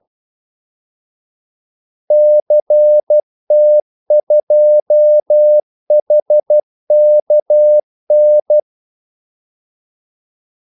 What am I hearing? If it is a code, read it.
Morse code, heard as a keyed tone, about 12 wpm, 600 Hz. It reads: CT2HKN
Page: 2 ekstra ordmellemrum mellem kaldesignalerne